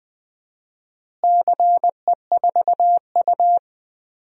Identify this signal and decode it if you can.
Morse code, heard as a keyed tone, about 20 wpm, 700 Hz. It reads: CE4U
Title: DXCC-100 Call Signs - 4 Characters - Repeated 2x - 20wpm